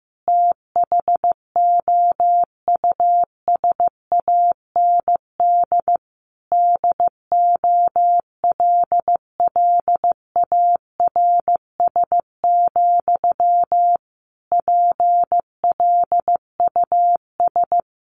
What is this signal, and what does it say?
Morse code, heard as a keyed tone, about 15 wpm, 700 Hz. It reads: THOUSAND DOLLARS, PLUS
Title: A Sound Of Thunder 15wpm